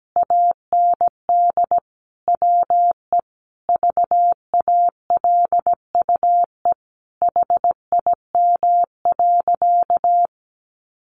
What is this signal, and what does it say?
Morse code, heard as a keyed tone, about 17 wpm, 700 Hz. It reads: AND WE VALUE HIM.
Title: Sentences from Top 600 Words 17wpm